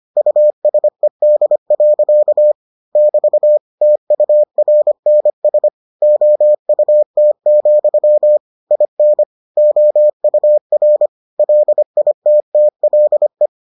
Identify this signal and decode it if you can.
Morse code, heard as a keyed tone, about 25 wpm, 600 Hz. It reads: USED. <BT> TURNS OUT, IN OUR LITTLE